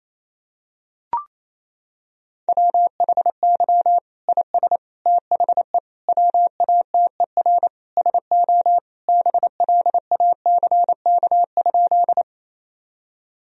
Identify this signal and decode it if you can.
Morse code, heard as a keyed tone, about 28 wpm, 700 Hz. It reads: WHY IS THE WATER SO BLACK?
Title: Sentences from Top 1000 Words - Review Part 2 - 28wpm